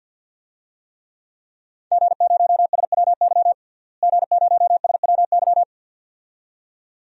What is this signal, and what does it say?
Morse code, heard as a keyed tone, about 52 wpm, 700 Hz. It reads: G0SWY G0SWY
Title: DXCC-100 Call Signs - 5 Characters - Repeated 2x - 50wpm